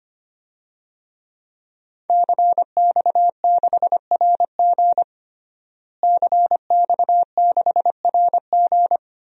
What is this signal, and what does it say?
Morse code, heard as a keyed tone, about 25 wpm, 700 Hz. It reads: CX6RG CX6RG
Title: DXCC-100 Call Signs - 5 Characters - Repeated 2x - 25wpm